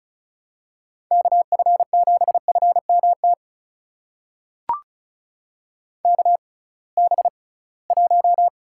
Morse code, heard as a keyed tone, about 34 wpm, spaced as for 22 wpm, 700 Hz. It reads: KF7FMT KB1
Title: USA Call Signs - 6 Characters - 6x Character Spacing - 35wpm